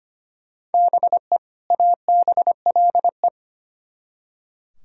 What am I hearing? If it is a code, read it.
Morse code, heard as a keyed tone, about 33 wpm, 700 Hz. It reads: BE ABLE